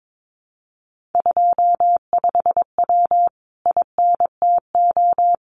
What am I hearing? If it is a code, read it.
Morse code, heard as a keyed tone, about 22 wpm, 700 Hz. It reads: 25W INTO
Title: Common QSO Phrases Encore 22wpm